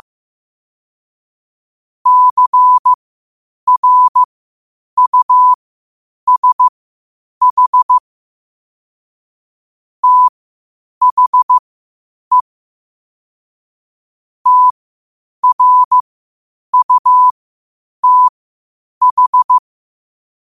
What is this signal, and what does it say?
Morse code, heard as a keyed tone, about 15 wpm, spaced as for 8 wpm, 1000 Hz. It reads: CRUSH THE TRUTH
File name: qotdfiles / 8wpm